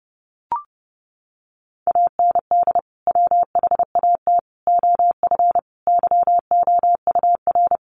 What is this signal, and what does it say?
Morse code, heard as a keyed tone, about 30 wpm, 700 Hz. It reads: AND WHAT OF YOUR
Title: Sentences from Top 900 Words 30wpm